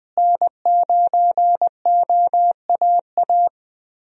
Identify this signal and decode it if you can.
Morse code, heard as a keyed tone, about 21 wpm, 700 Hz. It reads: N9OAA
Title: USA Call Signs - 5 Characters - Repeated 2x - 20wpm